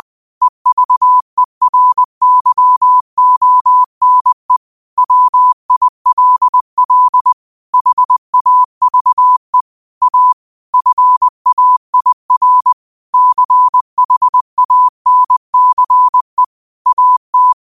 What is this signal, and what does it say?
Morse code, heard as a keyed tone, about 20 wpm, 1000 Hz. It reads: EVERYONE WILL HAVE A FAIR CHANCE AT
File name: qotdfiles / 20wpm